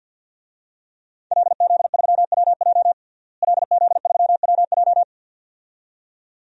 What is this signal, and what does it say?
Morse code, heard as a keyed tone, about 50 wpm, 700 Hz. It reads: LZ3WJ LZ3WJ
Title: DXCC-100 Call Signs - 5 Characters - Repeated 2x - 50wpm